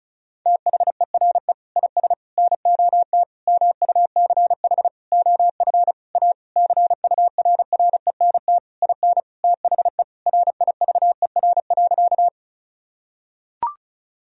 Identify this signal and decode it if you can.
Morse code, heard as a keyed tone, about 36 wpm, 700 Hz. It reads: THERE IS NOT MUCH OF A CURRENT IN THE RIVER.
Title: Sentences from Top 1000 Words - Review Part 4 - 35wpm